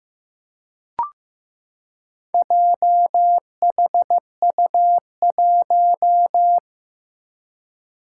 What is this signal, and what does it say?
Morse code, heard as a keyed tone, about 15 wpm, 700 Hz. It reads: JHU1
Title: Four Challenging Letter-Number Combinations 15wpm